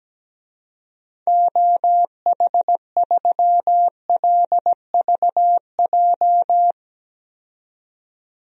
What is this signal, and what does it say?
Morse code, heard as a keyed tone, about 17 wpm, 700 Hz. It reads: OH3LVJ
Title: DXCC-100 Call Signs - 6 Characters - 17wpm